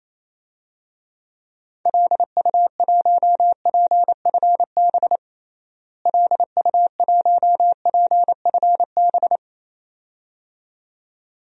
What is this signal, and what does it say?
Morse code, heard as a keyed tone, about 28 wpm, 700 Hz. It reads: LU1PFB LU1PFB
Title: DXCC-100 Call Signs - 6 Characters - Repeated 2x - 28wpm